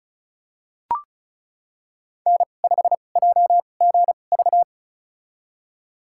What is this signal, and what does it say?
Morse code, heard as a keyed tone, about 36 wpm, spaced as for 25 wpm, 700 Hz. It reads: N5JGV
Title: USA Call Signs - 2x Character Spacing - 35wpm